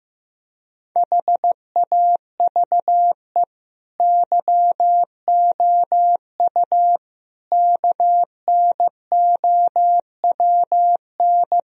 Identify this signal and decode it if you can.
Morse code, heard as a keyed tone, about 16 wpm, 700 Hz. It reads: HAVE YOU KNOWN
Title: Sentences from Top 1000 Words - Review Part 3 - 15wpm